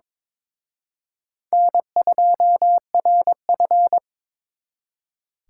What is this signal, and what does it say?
Morse code, heard as a keyed tone, about 26 wpm, 700 Hz. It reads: N2RF
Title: Speed Racing - USA 1x2 Call Signs 22wpm